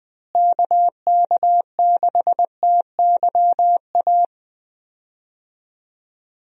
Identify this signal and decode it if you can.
Morse code, heard as a keyed tone, about 21 wpm, 700 Hz. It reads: KK6TYA
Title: USA Call Signs - 6 Characters - Repeated 2x - 20wpm